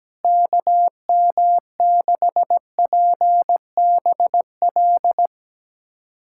Morse code, heard as a keyed tone, about 17 wpm, 700 Hz. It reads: KM6PBL
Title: USA Callsign DE Callsign - 6 Characters - 17wpm